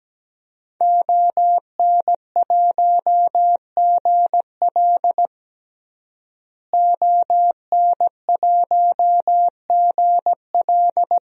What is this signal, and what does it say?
Morse code, heard as a keyed tone, about 17 wpm, 700 Hz. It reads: ON1GL ON1GL
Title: DXCC-100 Call Signs - 5 Characters - Repeated 2x - 17wpm